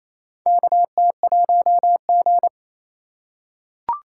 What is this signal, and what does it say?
Morse code, heard as a keyed tone, about 28 wpm, 700 Hz. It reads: KT1G
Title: USA Call Signs - 4 Characters - 2x Character Spacing - 28wpm